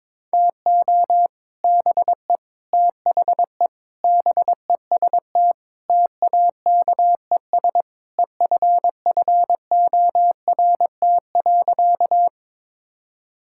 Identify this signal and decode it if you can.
Morse code, heard as a keyed tone, about 22 wpm, 700 Hz. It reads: TO BE THE BEST TAKES EFFORT.
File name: practice / 22wpm / Sentences with 3-Word Phrases - 22wpm